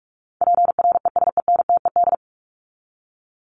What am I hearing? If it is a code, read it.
Morse code, heard as a keyed tone, about 44 wpm, 700 Hz. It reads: PRESENTED
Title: ED Suffix - Top 100 Words with Base Words 45wpm